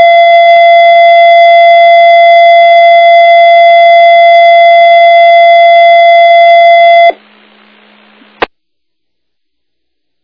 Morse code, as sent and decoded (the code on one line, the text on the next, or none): none